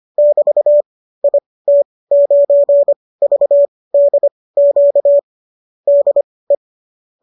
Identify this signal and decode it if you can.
Morse code, heard as a keyed tone, about 25 wpm, spaced as for 18 wpm, 590 Hz. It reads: <BT> IT9VDQ DE